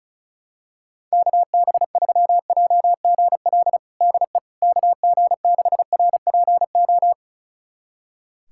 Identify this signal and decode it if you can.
Morse code, heard as a keyed tone, about 36 wpm, 700 Hz. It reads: KB3JGL DE KG6RPO
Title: USA Callsign DE Callsign 35wpm